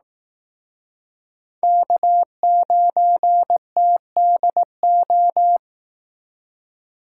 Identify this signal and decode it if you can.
Morse code, heard as a keyed tone, about 18 wpm, 700 Hz. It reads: K9TDO
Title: Speed Racing - USA 1x3 Call Signs 15wpm